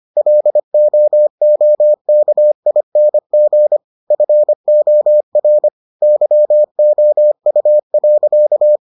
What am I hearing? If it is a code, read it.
Morse code, heard as a keyed tone, about 25 wpm, 600 Hz. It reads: LOOKING FOR YOU.